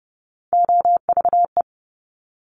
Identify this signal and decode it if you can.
Morse code, heard as a keyed tone, about 30 wpm, 690 Hz. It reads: OVE
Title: Top Three Letter Combinations in Words 30wpm